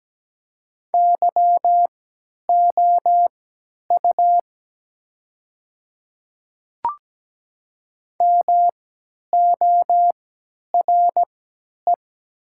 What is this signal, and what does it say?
Morse code, heard as a keyed tone, about 17 wpm, spaced as for 10 wpm, 700 Hz. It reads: YOU MORE
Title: 3x Character Spacing - Top 200 Words - 17wpm